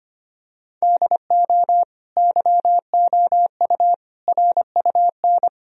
Morse code, heard as a keyed tone, about 25 wpm, 700 Hz. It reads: DO YOU RUN
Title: Sentences from Top 1000 Words - Review Part 4 - 25wpm